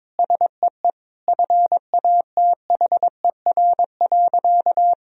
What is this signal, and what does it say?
Morse code, heard as a keyed tone, about 22 wpm, 700 Hz. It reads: SEE FATHER.
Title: Sentences from Top 1000 Words - Review Part 1 - 22wpm